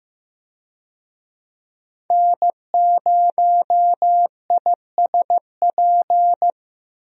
Morse code, heard as a keyed tone, about 15 wpm, 700 Hz. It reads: N0ISP
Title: USA Call Signs - 5 Characters - Repeated 3x - 15wpm